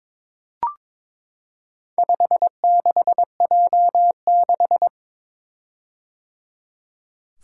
Morse code, heard as a keyed tone, about 22 wpm, 700 Hz. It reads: 56J6
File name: practice / 22wpm / Four Letter-Number Combinations - 22wpm